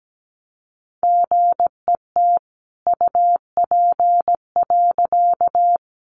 Morse code, heard as a keyed tone, about 17 wpm, 700 Hz. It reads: GET UP.
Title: A Sound Of Thunder 17wpm